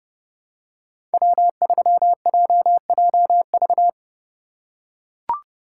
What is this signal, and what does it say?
Morse code, heard as a keyed tone, about 30 wpm, 700 Hz. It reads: W3JJV
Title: USA Call Signs - 5 Characters - 3x Character Spacing - 30wpm